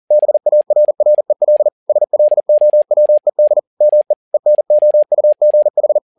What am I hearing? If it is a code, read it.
Morse code, heard as a keyed tone, about 40 wpm, 600 Hz. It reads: BARREL SLOWED ME ENOUGH
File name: learning center / tower trouble / tower touble 40wpm